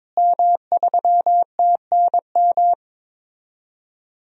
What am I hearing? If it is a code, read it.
Morse code, heard as a keyed tone, about 22 wpm, 700 Hz. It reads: M3TNM
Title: DXCC-100 Call Signs - 5 Characters - Repeated 2x - 22wpm